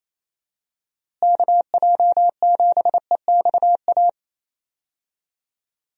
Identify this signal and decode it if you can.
Morse code, heard as a keyed tone, about 28 wpm, 700 Hz. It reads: KJ7EXA
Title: USA Call Signs - 6 Characters - Repeated 2x - 28wpm